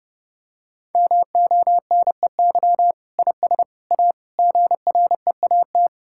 Morse code, heard as a keyed tone, about 32 wpm, 700 Hz. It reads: MONEY IS A GREAT
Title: Sentences from Top 1600 Words 30wpm